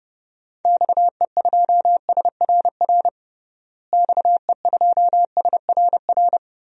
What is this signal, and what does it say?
Morse code, heard as a keyed tone, about 32 wpm, 700 Hz. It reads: XE2SRR XE2SRR
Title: DXCC-100 Call Signs - 6 Characters - Repeated 3x - 30wpm